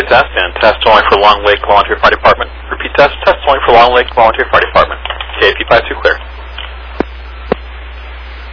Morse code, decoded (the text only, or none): none